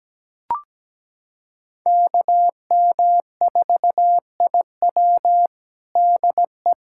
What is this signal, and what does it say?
Morse code, heard as a keyed tone, about 17 wpm, 700 Hz. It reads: KM4IW DE
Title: USA Callsign DE Callsign - 5 Characters - 17wpm